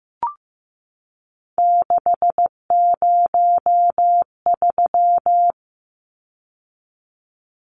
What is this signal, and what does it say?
Morse code, heard as a keyed tone, about 15 wpm, 700 Hz. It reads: 603
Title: Numbers - 3-Digit 15wpm